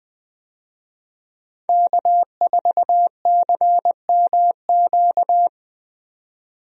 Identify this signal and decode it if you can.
Morse code, heard as a keyed tone, about 20 wpm, 700 Hz. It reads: K4CMQ
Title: USA Call Signs - 5 Characters - 20wpm